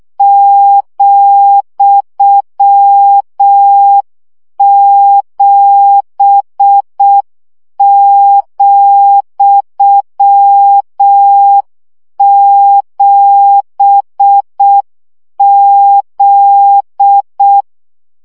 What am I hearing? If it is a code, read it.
Morse code, heard as a keyed tone, about 6 wpm, 800 Hz. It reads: ,7,7Z
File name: CW gyakorlas z 7 vesszo 5 baud